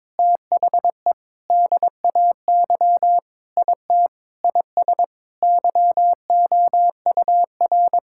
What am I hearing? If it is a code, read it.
Morse code, heard as a keyed tone, about 22 wpm, 700 Hz. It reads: THE DAY IT IS YOUR
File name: practice / 22wpm / Sentences with 8-Word Phrases - 22wpm